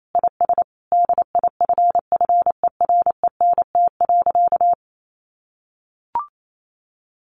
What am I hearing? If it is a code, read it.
Morse code, heard as a keyed tone, about 28 wpm, 700 Hz. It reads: IS DIFFERENT.
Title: Sentences from Top 1400 Words 28wpm